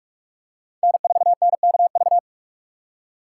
Morse code, heard as a keyed tone, about 42 wpm, 690 Hz. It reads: N4NKV
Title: USA Call Signs - 5 Characters - 6x Character Spacing - 45wpm